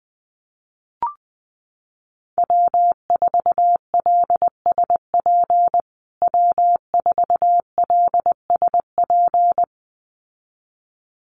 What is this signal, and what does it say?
Morse code, heard as a keyed tone, about 20 wpm, 700 Hz. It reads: W4LSP W4LSP
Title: USA Call Signs Repeated 2x 20wpm